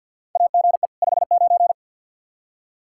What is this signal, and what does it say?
Morse code, heard as a keyed tone, about 49 wpm, 700 Hz. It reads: AGE 59
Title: Practice QSO - Exchange 3 - More Info - 50wpm